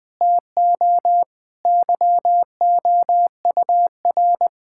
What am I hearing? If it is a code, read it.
Morse code, heard as a keyed tone, about 20 wpm, 700 Hz. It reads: TO YOUR
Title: Sentences from Top 1600 Words 20wpm